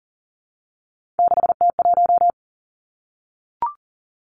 Morse code, heard as a keyed tone, about 38 wpm, 700 Hz. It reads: 6T1
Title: Three Challenging Letter-Number Combinations 40wpm